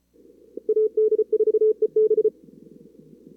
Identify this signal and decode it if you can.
Morse code, heard as a keyed tone, about 34 wpm, 440 Hz. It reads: AD4EB
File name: CWSprint / Feb2022 / k5tr-feb-na-sprint-cw-k5ot -2300